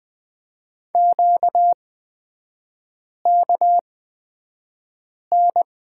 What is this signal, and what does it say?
Morse code, heard as a keyed tone, about 20 wpm, spaced as for 5 wpm, 700 Hz. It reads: QKN
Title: 255 Lesson 41e - Period - Three Characters 20wpm